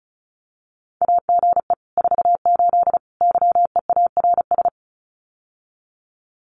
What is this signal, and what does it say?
Morse code, heard as a keyed tone, about 36 wpm, 700 Hz. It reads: AGE 48 YEARS
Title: Common QSO Phrases Encore 35wpm